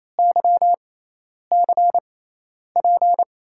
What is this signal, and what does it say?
Morse code, heard as a keyed tone, about 28 wpm, spaced as for 10 wpm, 700 Hz. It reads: YCP
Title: USA Call Signs - 6 Characters - 6x Character Spacing - 28wpm